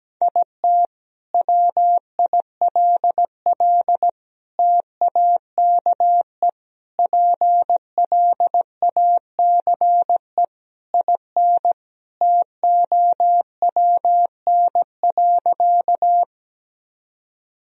Morse code, heard as a keyed tone, about 17 wpm, 700 Hz. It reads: IT WILL TAKE PLACE IN TOWN.
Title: Sentences from Top 1000 Words - Review Part 2 - 17wpm